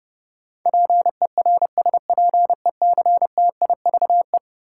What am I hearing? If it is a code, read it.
Morse code, heard as a keyed tone, about 30 wpm, 700 Hz. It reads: PERSPECTIVE
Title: Sentences from Top 1200 Words - New Word Drill Alt - 30wpm